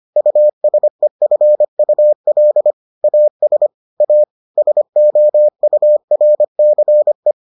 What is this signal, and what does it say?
Morse code, heard as a keyed tone, about 25 wpm, 600 Hz. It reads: USEFUL AS A SOURCE